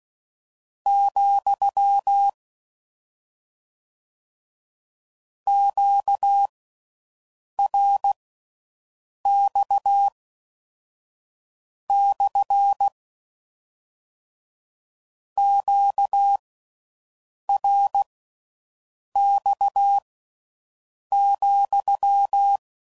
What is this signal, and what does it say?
Morse code, heard as a keyed tone, about 16 wpm, spaced as for 6 wpm, 800 Hz. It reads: , QRX / QRX,